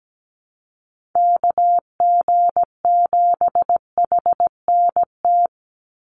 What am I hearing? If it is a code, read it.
Morse code, heard as a keyed tone, about 17 wpm, 700 Hz. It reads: KG7HNT
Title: USA Call Signs 17wpm